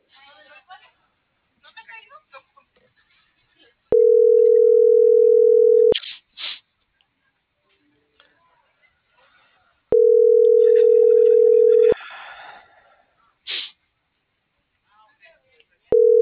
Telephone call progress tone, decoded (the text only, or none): ringback tone